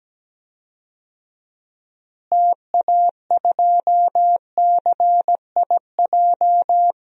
Morse code, heard as a keyed tone, about 17 wpm, 700 Hz. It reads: TA2CIJ
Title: DXCC-100 Call Signs - 6 Characters - Repeated 3x - 17wpm